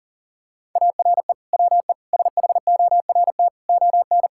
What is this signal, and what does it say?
Morse code, heard as a keyed tone, about 37 wpm, 700 Hz. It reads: ARE WE SHORT ON
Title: Sentences from Top 1000 Words - Review Part 5 - 40wpm